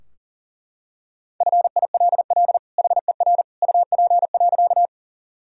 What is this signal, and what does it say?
Morse code, heard as a keyed tone, about 40 wpm, 700 Hz. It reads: FILL HER UP.
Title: Sentences from Top 1000 Words - Review Part 3 - 40wpm